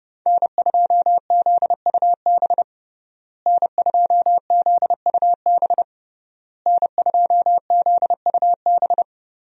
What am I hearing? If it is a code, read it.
Morse code, heard as a keyed tone, about 30 wpm, 700 Hz. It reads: N2ZUB N2ZUB N2ZUB
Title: USA Call Signs - 5 Characters - Repeated 3x - 30wpm